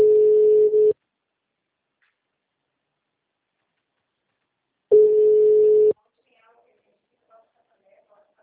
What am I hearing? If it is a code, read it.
Ringback tone (Europe, 425 Hz), with speech.
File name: out- -2044- - - .25823